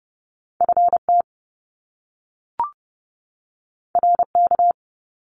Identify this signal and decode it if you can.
Morse code, heard as a keyed tone, about 30 wpm, 700 Hz. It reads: FT RK
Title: Top Two Letter Combinations in Words 30wpm